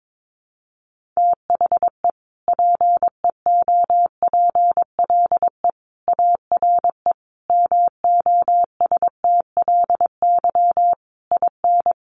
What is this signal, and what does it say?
Morse code, heard as a keyed tone, about 22 wpm, 700 Hz. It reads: THE PEOPLE ARE MOSTLY IN
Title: Sentences from Top 1300 Words 22wpm